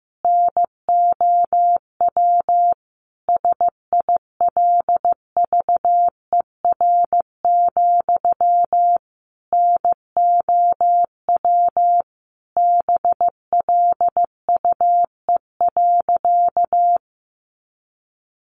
Morse code, heard as a keyed tone, about 15 wpm, 700 Hz. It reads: NOW SILVER, NOW BLUE.
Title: A Sound Of Thunder 15wpm